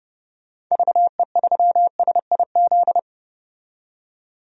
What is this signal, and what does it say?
Morse code, heard as a keyed tone, about 30 wpm, 700 Hz. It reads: VE3SIZ
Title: DXCC-100 Call Signs - 6 Characters - 30wpm